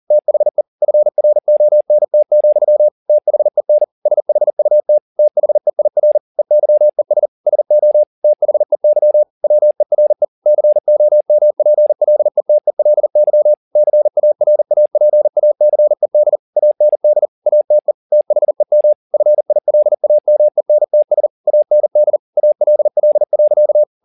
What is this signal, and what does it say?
Morse code, heard as a keyed tone, about 40 wpm, 600 Hz. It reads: THE FRONT, THEN SHUT THEIR EYES SO THEY WERE COMPLETELY CARAPACED AND ATE THEM FILAMENTS AND ALL.